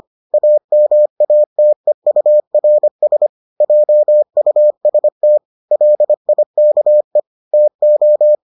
Morse code, heard as a keyed tone, about 25 wpm, 600 Hz. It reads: AMATEURS JUST LIKE TO